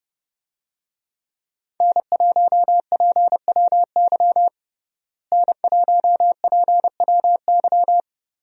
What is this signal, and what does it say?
Morse code, heard as a keyed tone, about 30 wpm, 700 Hz. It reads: N1PWY N1PWY
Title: USA Call Signs - 5 Characters - Repeated 2x - 30wpm